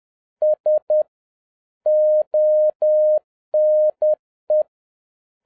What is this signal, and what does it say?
Morse code, heard as a keyed tone, about 10 wpm, 610 Hz. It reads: S ONE
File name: 2011-07-mp3 / n31 THE 3.11 2 10wpm